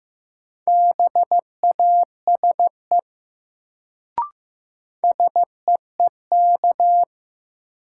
Morse code, heard as a keyed tone, about 15 wpm, 700 Hz. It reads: BASE SEEK
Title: Sentences from Top 1600 Words - New Word Drill Alt - 15wpm